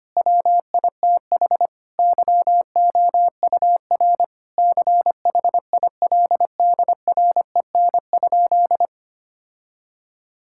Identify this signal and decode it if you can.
Morse code, heard as a keyed tone, about 25 wpm, 700 Hz. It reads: WITH YOUR CHILDREN?
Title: Sentences from Top 1000 Words - Review Part 6 - 25wpm